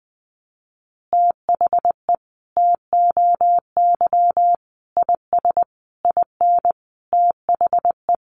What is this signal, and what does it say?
Morse code, heard as a keyed tone, about 20 wpm, 700 Hz. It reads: THE TOY IS IN THE
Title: Easy Phrases 20wpm